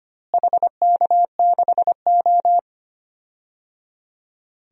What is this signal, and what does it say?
Morse code, heard as a keyed tone, about 25 wpm, 700 Hz. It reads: HK6O
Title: DXCC-100 Call Signs - 4 Characters - Repeated 3x - 25wpm